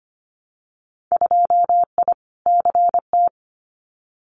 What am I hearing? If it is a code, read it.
Morse code, heard as a keyed tone, about 25 wpm, 690 Hz. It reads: 2I CT
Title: Winter Field Day Exchanges 25wpm